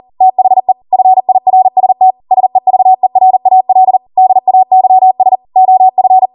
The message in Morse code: THE FIRST SEVERAL DAYS OF

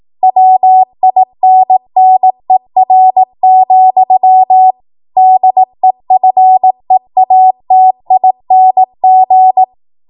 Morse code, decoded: WINNER, DEFEATING

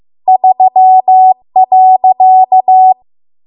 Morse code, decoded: 3.

